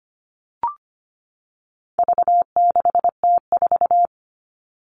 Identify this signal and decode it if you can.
Morse code, heard as a keyed tone, about 25 wpm, 700 Hz. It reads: V6T4
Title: All Cut Numbers - 4-Digit 25wpm